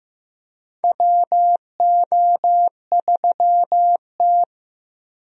Morse code, heard as a keyed tone, about 15 wpm, 700 Hz. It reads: WO3T
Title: USA Call Signs - 4 Characters - 15wpm